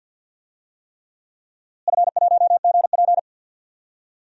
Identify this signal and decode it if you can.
Morse code, heard as a keyed tone, about 48 wpm, 700 Hz. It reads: F1GP
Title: DXCC-100 Call Signs - 4 Characters - Repeated 3x - 50wpm